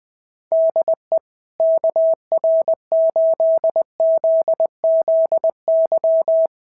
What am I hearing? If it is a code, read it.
Morse code, heard as a keyed tone, about 21 wpm, 690 Hz. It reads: DE KR8ZZY
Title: SOTA QSO Practice 20wpm